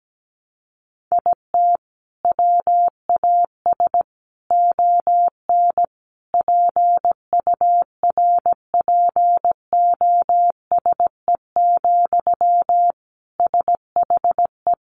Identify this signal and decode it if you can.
Morse code, heard as a keyed tone, about 17 wpm, 700 Hz. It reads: IT WAS ON PURPOSE, SHE
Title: Sentences from Top 800 Words 17wpm